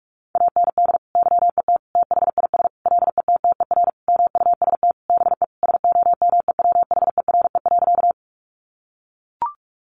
Morse code, heard as a keyed tone, about 44 wpm, 700 Hz. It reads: AND YET THIS LETTER MUST BE SOMEWHERE.
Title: Sentences from Top 1400 Words 45wpm